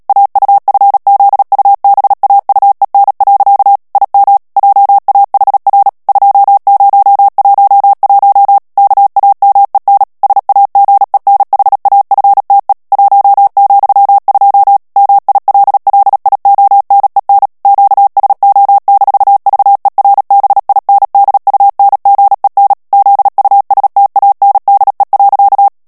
Morse code, heard as a keyed tone, about 37 wpm, 800 Hz. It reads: AUFZUBAUEN. IM JAHR 2011 KAMEN SAGENHAFTE 1,2 MILLIONEN QSO-VERBINDUNGEN ZUSTANDE.
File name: morse mp3 / ALLEMAND texte clair 2 vitesse 35 mots minute